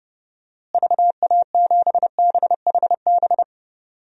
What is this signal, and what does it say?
Morse code, heard as a keyed tone, about 32 wpm, 700 Hz. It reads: VA7BHB
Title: DXCC-100 Call Signs - 6 Characters - Repeated 3x - 30wpm